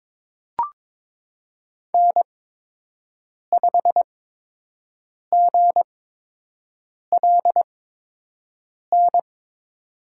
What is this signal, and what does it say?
Morse code, heard as a keyed tone, about 22 wpm, spaced as for 6 wpm, 700 Hz. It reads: N5GLN
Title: USA Call Signs - 5 Characters - 8x Character Spacing - 22wpm